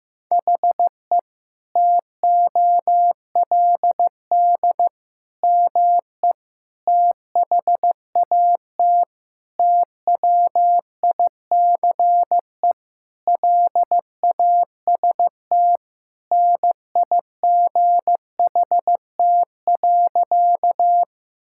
Morse code, heard as a keyed tone, about 15 wpm, 700 Hz. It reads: HE TOLD ME THAT TWICE LAST NIGHT.
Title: Sentences from Top 1600 Words 15wpm